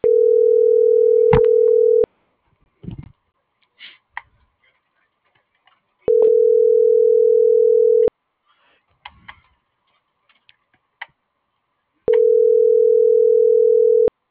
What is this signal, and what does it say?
Ringback tone (North America), 440 + 480 Hz.
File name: CENTRAL GENERAL / out karla.sky - .73335